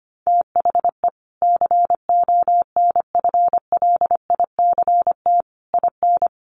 Morse code, heard as a keyed tone, about 25 wpm, 700 Hz. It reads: THE CONFLICT IN
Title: Sentences from Top 1300 Words 25wpm